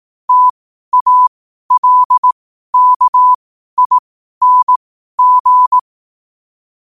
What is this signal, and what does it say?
Morse code, heard as a keyed tone, about 18 wpm, spaced as for 12 wpm, 1000 Hz. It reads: TALKING
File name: qotdfiles / 12wpm